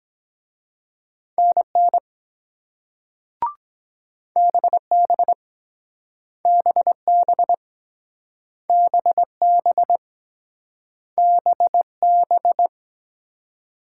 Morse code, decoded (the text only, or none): NN BB BB BB BB